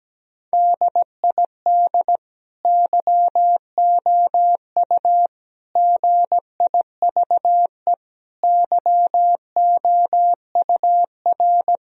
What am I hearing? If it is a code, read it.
Morse code, heard as a keyed tone, about 17 wpm, 700 Hz. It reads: DID YOU GIVE YOUR
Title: Sentences from Top 1000 Words - Review Part 4 - 17wpm